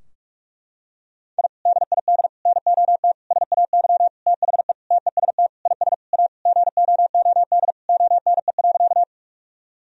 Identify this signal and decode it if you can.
Morse code, heard as a keyed tone, about 41 wpm, 700 Hz. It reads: I DID NOT SAY THE TEST IS A GOOD ONE.
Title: Sentences from Top 1000 Words - Review Part 3 - 45wpm